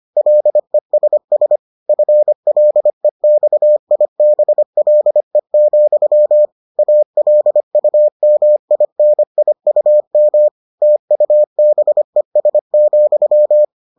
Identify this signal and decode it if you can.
Morse code, heard as a keyed tone, about 25 wpm, 600 Hz. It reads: LESS FLEXIBLE, ALUMINIUM TUBES,